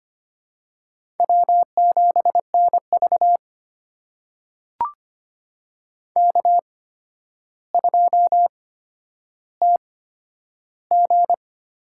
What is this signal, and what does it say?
Morse code, heard as a keyed tone, about 25 wpm, spaced as for 11 wpm, 700 Hz. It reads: W7NV K2TG
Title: USA Call Signs - 4 Characters - 8x Character Spacing - 25wpm